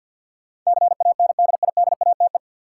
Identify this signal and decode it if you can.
Morse code, heard as a keyed tone, about 50 wpm, 700 Hz. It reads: CANDIDATE
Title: Sentences from Top 1600 Words - New Word Drill - 50wpm